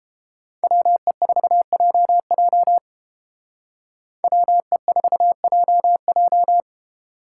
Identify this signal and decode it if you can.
Morse code, heard as a keyed tone, about 29 wpm, 700 Hz. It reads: WE4JJ WE4JJ